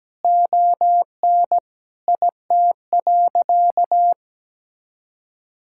Morse code, heard as a keyed tone, about 17 wpm, 700 Hz. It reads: ON IT.
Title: Sentences from Top 1600 Words 17wpm